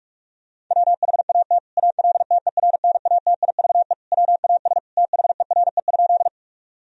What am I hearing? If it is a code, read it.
Morse code, heard as a keyed tone, about 48 wpm, 700 Hz. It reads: WHAT ALTERNATIVE WAS THERE?
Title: Sentences from Top 1000 Words - Review Part 4 - 45wpm